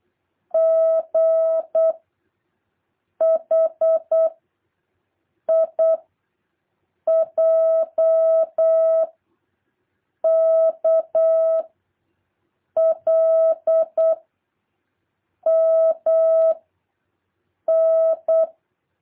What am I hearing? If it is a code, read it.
Morse code, heard as a keyed tone, about 8 wpm, 650 Hz. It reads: G H I J K L M N
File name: jh8cbh / 2023cw